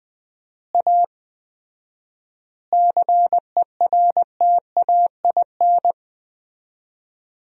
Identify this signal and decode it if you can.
Morse code, heard as a keyed tone, about 20 wpm, 700 Hz. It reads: A CERTAIN
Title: 4x Word-Spacing - Sentences from Top 500 Words - 20wpm